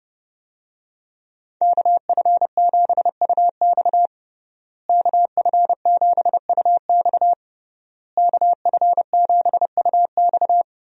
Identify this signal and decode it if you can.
Morse code, heard as a keyed tone, about 32 wpm, 700 Hz. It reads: KF7UX KF7UX KF7UX
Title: USA Call Signs - 5 Characters - Repeated 3x - 30wpm